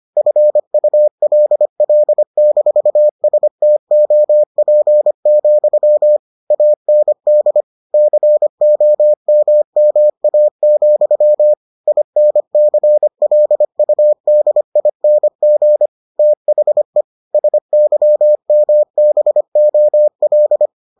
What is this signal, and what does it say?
Morse code, heard as a keyed tone, about 25 wpm, 600 Hz. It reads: FULL-STOP, AND COMMA, INCLUDING THE SYMBOL